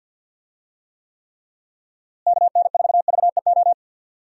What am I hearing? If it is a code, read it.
Morse code, heard as a keyed tone, about 50 wpm, 700 Hz. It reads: KN4VEO
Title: USA Call Signs - 6 Characters - Repeated 3x - 50wpm